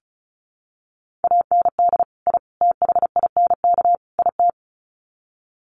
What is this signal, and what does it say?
Morse code, heard as a keyed tone, about 34 wpm, 700 Hz. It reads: AND I THINK IT
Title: Sets of 4 Words 35wpm